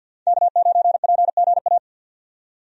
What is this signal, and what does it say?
Morse code, heard as a keyed tone, about 51 wpm, 700 Hz. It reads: K9PGA
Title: USA Call Signs - 5 Characters - 50wpm